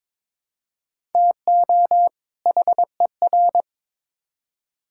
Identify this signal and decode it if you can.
Morse code, heard as a keyed tone, about 22 wpm, 700 Hz. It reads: TO HER
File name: Build A Phrase - 2-Word Phrases - 22wpm